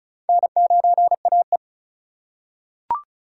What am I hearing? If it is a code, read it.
Morse code, heard as a keyed tone, about 36 wpm, 700 Hz. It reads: N9AE
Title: USA Call Signs - 2x Character Spacing - 35wpm